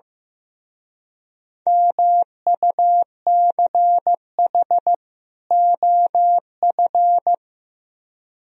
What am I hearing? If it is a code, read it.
Morse code, heard as a keyed tone, about 16 wpm, 700 Hz. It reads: MUCH OF
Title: Speed-Racing - Sets of 2 Words 15wpm